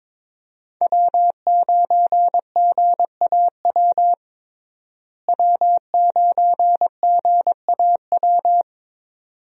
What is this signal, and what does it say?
Morse code, heard as a keyed tone, about 22 wpm, 700 Hz. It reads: W9GAW W9GAW
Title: USA Call Signs - 5 Characters - Repeated 2x - 22wpm